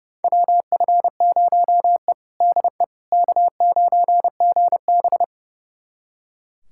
Morse code, heard as a keyed tone, about 30 wpm, 700 Hz. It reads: WF0E DE K9GB
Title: USA Callsign DE Callsign - 4 Characters - 30wpm